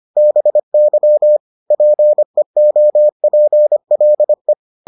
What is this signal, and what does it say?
Morse code, heard as a keyed tone, about 25 wpm, 600 Hz. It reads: BY PEOPLE